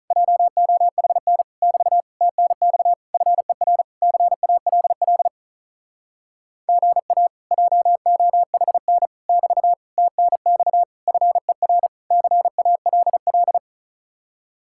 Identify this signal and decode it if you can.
Morse code, heard as a keyed tone, about 46 wpm, 700 Hz. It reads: JOHN <BT> TNX FER CALL GA JOHN <BT> TNX FER CALL